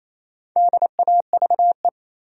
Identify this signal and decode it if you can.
Morse code, heard as a keyed tone, about 28 wpm, 700 Hz. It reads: DAVE
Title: USA Common Names - 28wpm